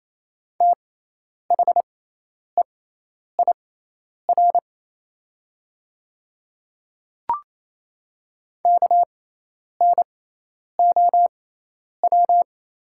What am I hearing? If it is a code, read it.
Morse code, heard as a keyed tone, about 28 wpm, spaced as for 10 wpm, 700 Hz. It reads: THEIR KNOW
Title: 6x Character Spacing - Top 100 Words - 28wpm